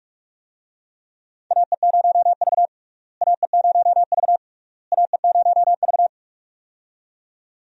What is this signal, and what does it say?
Morse code, heard as a keyed tone, about 46 wpm, 700 Hz. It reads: AE0V AE0V AE0V
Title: USA Call Signs - 4 Characters - Repeated 3x - 45wpm